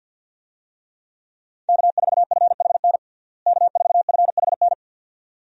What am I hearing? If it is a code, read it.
Morse code, heard as a keyed tone, about 53 wpm, 700 Hz. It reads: K4FHN K4FHN
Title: USA Call Signs - 5 Characters - Repeated 2x - 50wpm